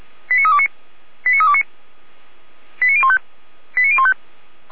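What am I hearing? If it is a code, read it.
Five-tone selective call (ZVEI), heard as quick runs of five tones, tones of about 70 ms. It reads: 89329 89329 89015 89015
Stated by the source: ZVEI five-tone selective call received via SDR.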